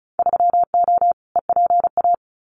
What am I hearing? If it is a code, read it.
Morse code, heard as a keyed tone, about 36 wpm, 700 Hz. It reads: 3O EPA
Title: Winter Field Day Exchanges 35wpm